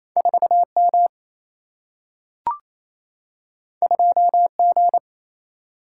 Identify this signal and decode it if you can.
Morse code, heard as a keyed tone, about 28 wpm, 700 Hz. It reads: 4M 2G
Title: Single-Letter Single-Number 28wpm